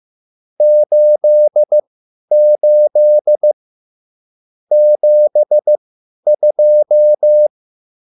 Morse code, heard as a keyed tone, about 15 wpm, spaced as for 10 wpm, 600 Hz. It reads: 88 72
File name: QSL / CW lesson